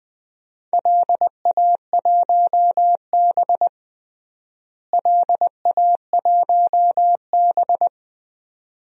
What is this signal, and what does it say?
Morse code, heard as a keyed tone, about 20 wpm, 700 Hz. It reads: LA1B LA1B
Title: DXCC-100 Call Signs - 4 Characters - Repeated 2x - 20wpm